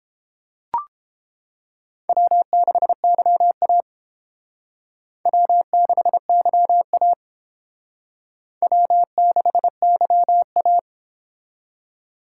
Morse code, heard as a keyed tone, about 28 wpm, 700 Hz. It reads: W6YA W6YA W6YA